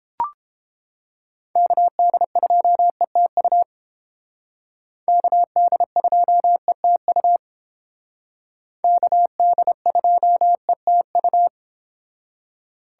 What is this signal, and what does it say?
Morse code, heard as a keyed tone, about 31 wpm, 700 Hz. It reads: KD2ETU KD2ETU KD2ETU